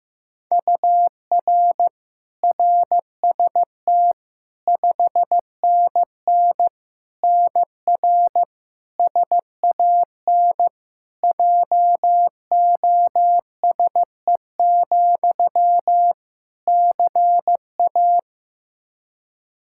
Morse code, heard as a keyed tone, about 15 wpm, 700 Hz. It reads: UR RST 5NN NR SAN JOSE, CA
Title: Practice QSO - Exchange 1 - Beginning - 15wpm